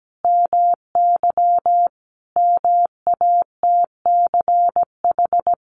Morse code, heard as a keyed tone, about 17 wpm, 700 Hz. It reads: MY MATCH
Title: A Sound Of Thunder 17wpm